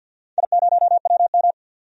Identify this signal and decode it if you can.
Morse code, heard as a keyed tone, about 47 wpm, 700 Hz. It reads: I0WM
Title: DXCC-100 Call Signs - 4 Characters - 50wpm